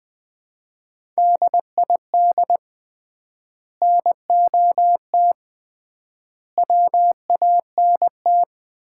Morse code, heard as a keyed tone, about 21 wpm, 700 Hz. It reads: DID NOT WANT
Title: 3x Word-Spacing - Sets of 4 Words - 20wpm